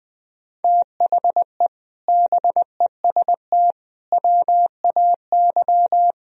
Morse code, heard as a keyed tone, about 22 wpm, 700 Hz. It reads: THE BEST WAY
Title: Speed-Racing - Sets of 4 Words 20wpm